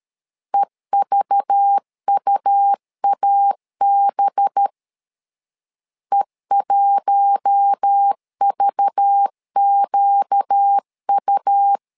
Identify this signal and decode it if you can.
Morse code, heard as a keyed tone, about 13 wpm, 800 Hz. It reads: EVUAB E1VQU
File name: LEZIONI / 5 LBVC / 5 LBV 12wpm 1855